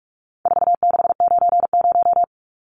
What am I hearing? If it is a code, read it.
Morse code, heard as a keyed tone, about 44 wpm, 700 Hz. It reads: 4690
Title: Numbers - 4-Digit 45wpm